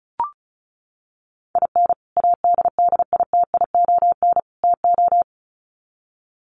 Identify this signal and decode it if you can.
Morse code, heard as a keyed tone, about 34 wpm, 700 Hz. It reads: IN ADDITION TO
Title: Sets of 3 Words 35wpm